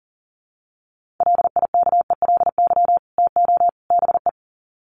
Morse code, heard as a keyed tone, about 40 wpm, 700 Hz. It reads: LIKELY TO BE
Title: Sets of 3 Words 40wpm